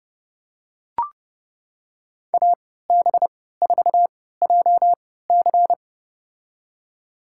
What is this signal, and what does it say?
Morse code, heard as a keyed tone, about 31 wpm, spaced as for 17 wpm, 700 Hz. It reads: AB4JC
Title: USA Call Signs - 3x Character Spacing - 30wpm